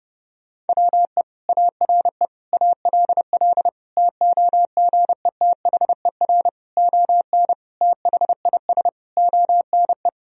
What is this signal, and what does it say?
Morse code, heard as a keyed tone, about 30 wpm, 700 Hz. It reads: WE ARE ALL TOGETHER ON THIS ONE
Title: Sentences from Top 1000 Words - Review Part 2 - 30wpm